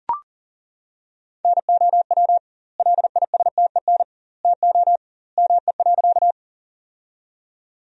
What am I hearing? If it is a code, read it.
Morse code, heard as a keyed tone, about 40 wpm, 700 Hz. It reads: NOW LISTEN TO ME.
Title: 2x Word-Spacing - Sentences from Top 400 Words - 40wpm